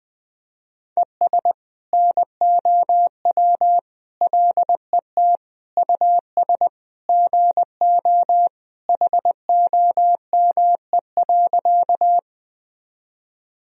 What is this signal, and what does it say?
Morse code, heard as a keyed tone, about 20 wpm, 700 Hz. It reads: ES NOW LET US GO HOME.
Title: Sentences from Top 300 Words and Abbreviations 20wpm